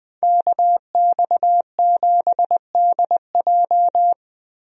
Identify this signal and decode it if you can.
Morse code, heard as a keyed tone, about 20 wpm, 700 Hz. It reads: KX7DJ
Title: USA Call Signs - 5 Characters - Repeated 3x - 20wpm